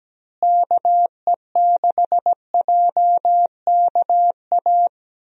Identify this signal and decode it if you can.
Morse code, heard as a keyed tone, about 17 wpm, 700 Hz. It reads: KE6JKA
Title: USA Call Signs - 6 Characters - Repeated 2x - 17wpm